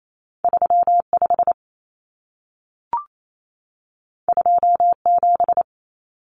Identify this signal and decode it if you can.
Morse code, heard as a keyed tone, about 28 wpm, 700 Hz. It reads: 35 27
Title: Common Cut Numbers - 2-Digit 28wpm